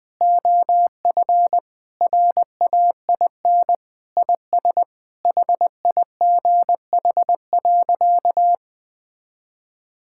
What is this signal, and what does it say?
Morse code, heard as a keyed tone, about 20 wpm, 700 Hz. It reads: OF RAIN IS HIGH.
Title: Sentences with 3-Word Phrases - 20wpm